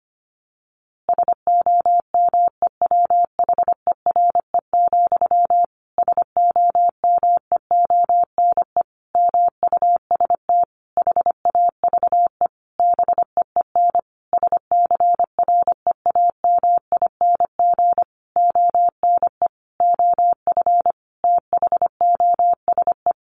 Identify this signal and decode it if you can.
Morse code, heard as a keyed tone, about 25 wpm, 700 Hz. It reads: SOMEWHERE, SOMEONE MUST HAVE BEEN SCREAMING ONE OF THOSE
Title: A Sound Of Thunder 25wpm